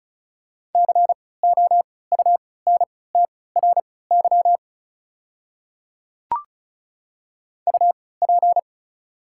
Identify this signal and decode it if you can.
Morse code, heard as a keyed tone, about 36 wpm, spaced as for 20 wpm, 700 Hz. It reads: COUNTRY UP
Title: 3x Character Spacing - Top 500 Words - 35wpm